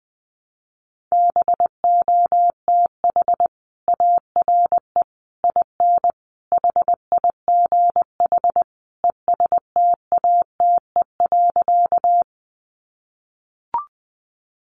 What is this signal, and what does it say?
Morse code, heard as a keyed tone, about 20 wpm, 700 Hz. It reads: BOTH ARE IN HIGH ESTATE.
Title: Sentences from Top 1200 Words 20wpm